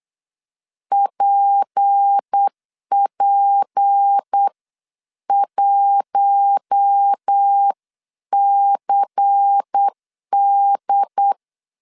Morse code, heard as a keyed tone, about 8 wpm, spaced as for 7 wpm, 800 Hz. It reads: PP1CD
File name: 8wpm